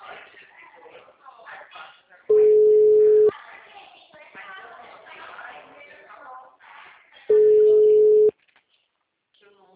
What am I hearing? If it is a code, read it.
Ringback tone (Europe, 425 Hz), with speech.